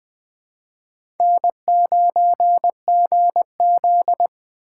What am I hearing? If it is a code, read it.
Morse code, heard as a keyed tone, about 20 wpm, 700 Hz. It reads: N9GZ
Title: 2023 POTA Hunter Call Signs - 20wpm